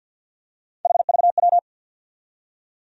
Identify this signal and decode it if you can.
Morse code, heard as a keyed tone, about 54 wpm, 700 Hz. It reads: HVW
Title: USA Call Sign Postfixes - 3 Letters 50wpm